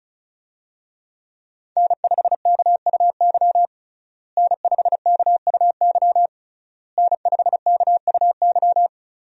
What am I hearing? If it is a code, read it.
Morse code, heard as a keyed tone, about 34 wpm, 700 Hz. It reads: N5KUY N5KUY N5KUY
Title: USA Call Signs - 5 Characters - Repeated 3x - 35wpm